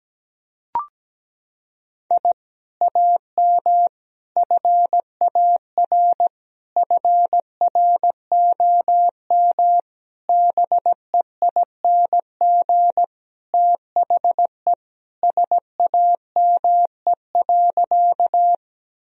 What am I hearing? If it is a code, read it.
Morse code, heard as a keyed tone, about 17 wpm, 700 Hz. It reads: I AM FAR FROM BEING THE SAME.
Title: Sentences from Top 1000 Words - Review Part 3 - 17wpm